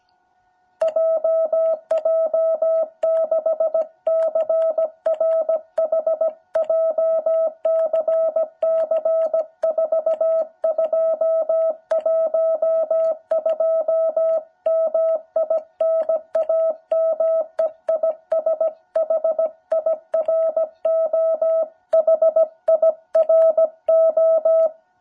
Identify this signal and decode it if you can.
Morse code, heard as a keyed tone, about 16 wpm, 660 Hz. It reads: JJ6CRH JCC4212 MI NAME IS HIRO HIRO
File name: jh8cbh / 2021cw